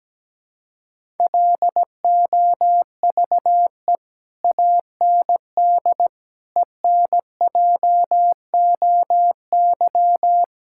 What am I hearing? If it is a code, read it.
Morse code, heard as a keyed tone, about 17 wpm, 700 Hz. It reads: LOVE AND ENJOY
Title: Sentences from Top 1000 Words - Review Part 5 - 17wpm